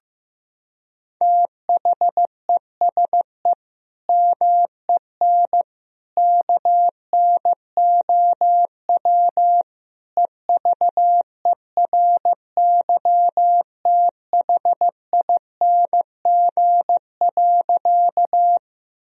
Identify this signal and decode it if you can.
Morse code, heard as a keyed tone, about 16 wpm, 700 Hz. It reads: THESE MEN KNOW EVERYTHING.
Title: Sentences from Top 1000 Words - Review Part 1 - 15wpm